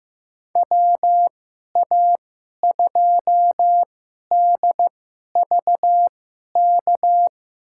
Morse code, heard as a keyed tone, about 15 wpm, spaced as for 11 wpm, 700 Hz. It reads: WA2DVK
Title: USA Call Signs - 6 Characters - 2x Character Spacing - 15wpm